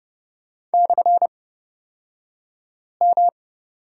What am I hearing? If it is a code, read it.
Morse code, heard as a keyed tone, about 30 wpm, spaced as for 5 wpm, 700 Hz. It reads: /M
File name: Lesson 37d - Letter x - Two Characters 30wpm